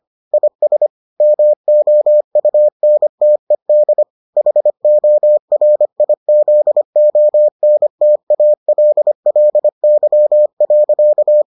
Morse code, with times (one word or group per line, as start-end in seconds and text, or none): 0.3-0.9 IS
1.2-4.0 MOUNTED
4.4-11.4 HORIZONTALLY.